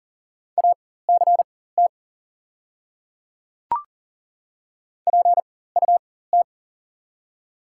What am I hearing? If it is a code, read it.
Morse code, heard as a keyed tone, about 37 wpm, spaced as for 18 wpm, 700 Hz. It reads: ACT PUT
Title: 4x Character Spacing - Top 500 Words - 40wpm